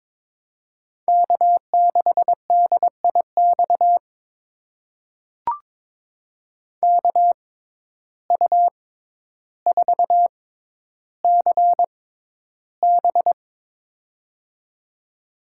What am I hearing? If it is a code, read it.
Morse code, heard as a keyed tone, about 22 wpm, spaced as for 11 wpm, 700 Hz. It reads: K6DIX KU4CB
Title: USA Call Signs - 5 Characters - 6x Character Spacing - 22wpm